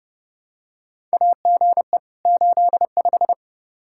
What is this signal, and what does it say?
Morse code, heard as a keyed tone, about 32 wpm, 700 Hz. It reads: AGE 85
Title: Practice QSO - Exchange 3 - More Info - 30wpm